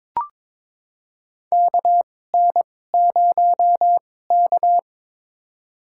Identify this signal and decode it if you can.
Morse code, heard as a keyed tone, about 22 wpm, spaced as for 16 wpm, 700 Hz. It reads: KN0K
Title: USA Call Signs - 4 Characters - 2x Character Spacing - 22wpm